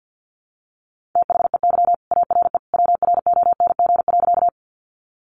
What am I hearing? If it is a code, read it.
Morse code, heard as a keyed tone, about 49 wpm, 700 Hz. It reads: THEY ARE WRONG.
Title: Sentences from Top 900 Words 50wpm